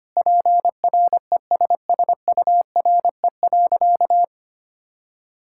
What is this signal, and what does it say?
Morse code, heard as a keyed tone, about 25 wpm, 700 Hz. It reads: PRESSURE.
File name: practice / 25wpm / All-Age Appropriate Jokes - 25wpm